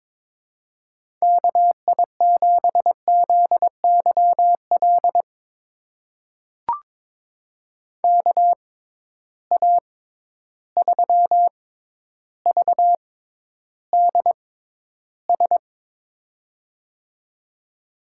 Morse code, heard as a keyed tone, about 22 wpm, spaced as for 11 wpm, 700 Hz. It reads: KI7ZYL KA3VDS
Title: USA Call Signs - 6 Characters - 6x Character Spacing - 22wpm